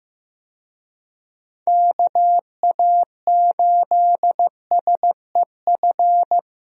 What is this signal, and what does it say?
Morse code, heard as a keyed tone, about 15 wpm, 700 Hz. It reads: KA8SEF
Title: USA Call Signs - 6 Characters - 15wpm